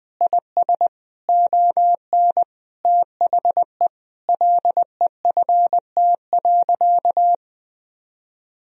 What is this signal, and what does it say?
Morse code, heard as a keyed tone, about 21 wpm, 700 Hz. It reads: IS ON THE LEFT.
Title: Sentences from Top 1000 Words - Review Part 4 - 20wpm